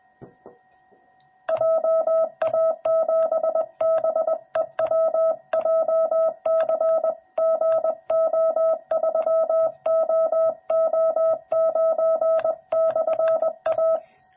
Morse code, heard as a keyed tone, about 20 wpm, 650 Hz. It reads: JA7BEW JCGO3OO9/A
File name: jh8cbh / 2021cw / cw06b